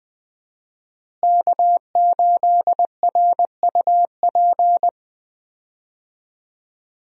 Morse code, heard as a keyed tone, about 20 wpm, 700 Hz. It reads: K8RUP
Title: USA Call Signs - 5 Characters - Repeated 3x - 20wpm